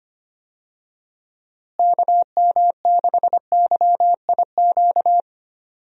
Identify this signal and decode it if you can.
Morse code, heard as a keyed tone, about 25 wpm, 700 Hz. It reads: KM6YIQ
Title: USA Call Signs - 6 Characters - Repeated 2x - 25wpm